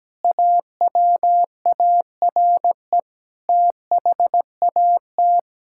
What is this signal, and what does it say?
Morse code, heard as a keyed tone, about 17 wpm, 700 Hz. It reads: AWARE THAT
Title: Sentences from Top 1600 Words 17wpm